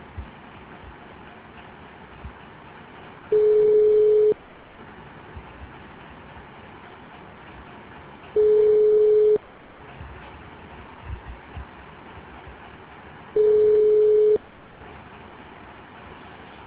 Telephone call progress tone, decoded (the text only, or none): ringback tone